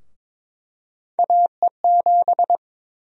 Morse code, heard as a keyed tone, about 22 wpm, 700 Hz. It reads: AE7
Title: USA Call Sign Prefixes - 22wpm